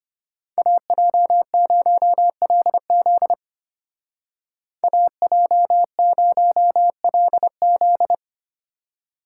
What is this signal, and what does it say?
Morse code, heard as a keyed tone, about 32 wpm, 700 Hz. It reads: AJ0LZ AJ0LZ